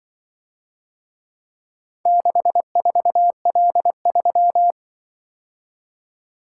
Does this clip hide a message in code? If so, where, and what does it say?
Yes: Morse code from 2.1 to 4.7 s, heard as a keyed tone, about 25 wpm, 700 Hz. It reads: 64L3